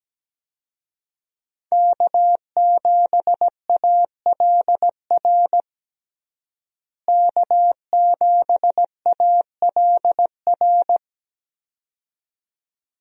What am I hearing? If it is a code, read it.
Morse code, heard as a keyed tone, about 17 wpm, 700 Hz. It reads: K7ALR K7ALR
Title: USA Call Signs - 5 Characters - Repeated 2x - 17wpm